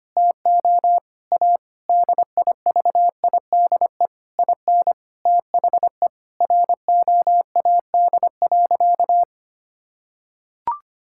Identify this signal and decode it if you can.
Morse code, heard as a keyed tone, about 25 wpm, 700 Hz. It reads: TO A DIVIDE IN THE ROAD.
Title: Sentences from Top 1600 Words 25wpm